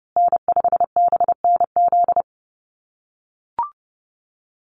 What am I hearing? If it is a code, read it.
Morse code, heard as a keyed tone, about 30 wpm, 700 Hz. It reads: N5BNZ
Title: USA Call Signs Repeated 3x 30wpm